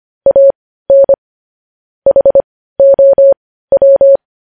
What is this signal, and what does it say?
Morse code, heard as a keyed tone, about 25 wpm, spaced as for 15 wpm, 550 Hz. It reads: AN HOW
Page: News headlines in Morse code at 15 wpm